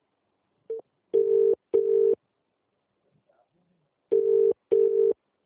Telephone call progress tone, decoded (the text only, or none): ringback tone